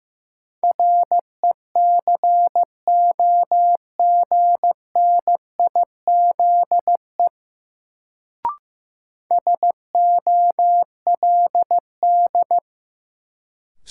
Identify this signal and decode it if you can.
Morse code, heard as a keyed tone, about 15 wpm, 700 Hz. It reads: RECOGNIZE SOLD
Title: Sentences from Top 1400 Words - New Word Drill - 15wpm